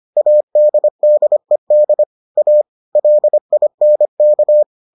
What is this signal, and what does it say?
Morse code, heard as a keyed tone, about 25 wpm, 600 Hz. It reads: ADDED A LINK